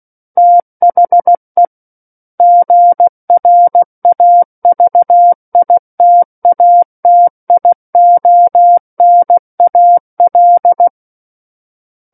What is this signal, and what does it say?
Morse code, heard as a keyed tone, about 16 wpm, 700 Hz. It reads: THE GRAVITATIONAL
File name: Book One / 07-Chapter 7 - 16 WPM